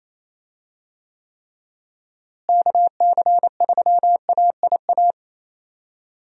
Morse code, heard as a keyed tone, about 27 wpm, 700 Hz. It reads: KC3AIA
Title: USA Call Signs - 6 Characters - Repeated 3x - 28wpm